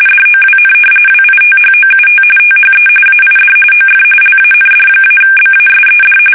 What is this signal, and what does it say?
RTTY, about 45.45 baud. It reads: DOWN WITHOUT THE AID OF LAND-BASED
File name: fsk850 / battleshipnj